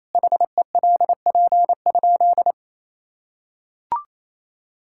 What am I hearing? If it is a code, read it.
Morse code, heard as a keyed tone, about 27 wpm, 700 Hz. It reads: HELP?
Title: Sentences from Top 1000 Words - Review Part 5 - 28wpm